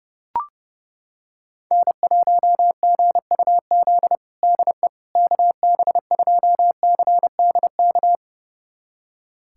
Morse code, heard as a keyed tone, about 30 wpm, 700 Hz. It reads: N1GUZ DE KB2CDK
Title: USA Callsign DE Callsign 30wpm